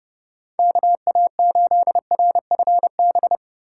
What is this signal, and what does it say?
Morse code, heard as a keyed tone, about 31 wpm, 700 Hz. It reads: KA8RFB